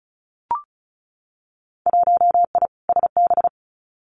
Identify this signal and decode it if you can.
Morse code, heard as a keyed tone, about 34 wpm, 700 Hz. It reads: 1I SB
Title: Winter Field Day Exchanges 35wpm